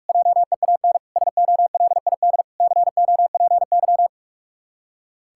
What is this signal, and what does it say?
Morse code, heard as a keyed tone, about 46 wpm, 700 Hz. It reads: JEAN SOLID COPY
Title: Practice QSO - Exchange 3 - More Info - 45wpm